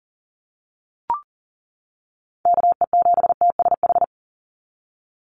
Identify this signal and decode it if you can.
Morse code, heard as a keyed tone, about 40 wpm, 700 Hz. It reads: KE7TSH
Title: USA Call Signs 40wpm